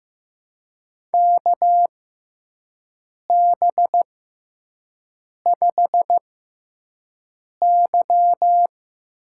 Morse code, heard as a keyed tone, about 15 wpm, spaced as for 5 wpm, 700 Hz. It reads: KB5Y
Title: USA Call Signs - 6 Characters - 6x Character Spacing - 15wpm